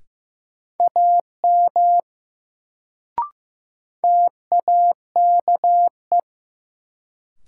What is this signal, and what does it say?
Morse code, heard as a keyed tone, about 15 wpm, 700 Hz. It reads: AM TAKE
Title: Sets of 2 Words - New Word Drill - 15wpm